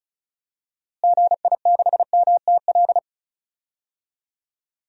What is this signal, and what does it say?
Morse code, heard as a keyed tone, about 36 wpm, 700 Hz. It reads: GI6MTL
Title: DXCC-100 Call Signs - 6 Characters - Repeated 3x - 35wpm